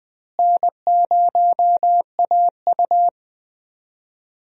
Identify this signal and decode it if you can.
Morse code, heard as a keyed tone, about 20 wpm, 700 Hz. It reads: N0AU
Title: USA Call Signs - 4 Characters - Repeated 3x - 20wpm